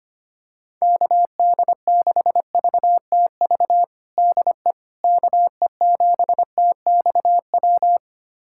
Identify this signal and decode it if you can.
Morse code, heard as a keyed tone, about 25 wpm, 700 Hz. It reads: KD6VTV DE KE7TXW
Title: USA Callsign DE Callsign 25wpm